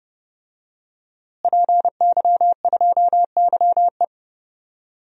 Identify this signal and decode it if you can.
Morse code, heard as a keyed tone, about 30 wpm, 700 Hz. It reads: PY2YE
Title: DXCC-100 Call Signs - 5 Characters - Repeated 2x - 30wpm